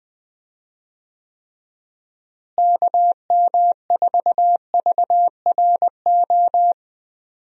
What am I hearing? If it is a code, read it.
Morse code, heard as a keyed tone, about 20 wpm, 700 Hz. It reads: KM4VRO
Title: USA Call Signs - 6 Characters - Repeated 3x - 20wpm